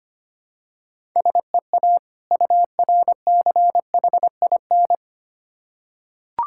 Morse code, heard as a keyed tone, about 25 wpm, 700 Hz. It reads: SEA URCHIN
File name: practice / 25wpm / Difficult Words with Hints - 25wpm